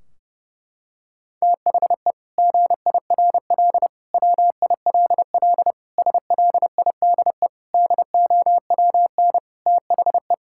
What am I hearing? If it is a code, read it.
Morse code, heard as a keyed tone, about 30 wpm, 700 Hz. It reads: THE GIRL WILL SLIDE DOWN THE
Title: Sentences from Top 1600 Words 30wpm